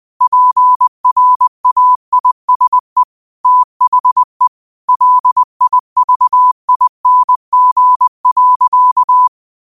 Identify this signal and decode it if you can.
Morse code, heard as a keyed tone, about 20 wpm, 1000 Hz. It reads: PRAISE THE LIVING.
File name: qotdfiles / 20wpm